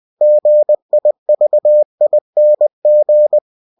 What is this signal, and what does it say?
Morse code, heard as a keyed tone, about 20 wpm, 600 Hz. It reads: GIVING